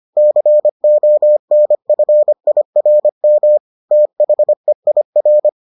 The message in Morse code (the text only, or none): CONFIRM THEIR